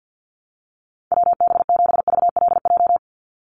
Morse code, heard as a keyed tone, about 50 wpm, 700 Hz. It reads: WB7VLJ
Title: USA Call Signs 50wpm